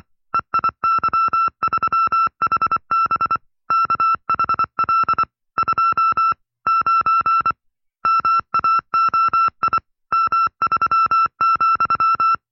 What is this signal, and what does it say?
Morse code, heard as a keyed tone, about 24 wpm, 1340 Hz. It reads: EIY3HB KHL 2 9 MAOI M3,
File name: FreeFiles / CW-practice / 20WPM-side2